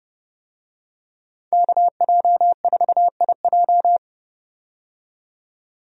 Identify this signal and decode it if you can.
Morse code, heard as a keyed tone, about 29 wpm, 700 Hz. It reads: KJ4IJ
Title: USA Call Signs - 5 Characters - 30wpm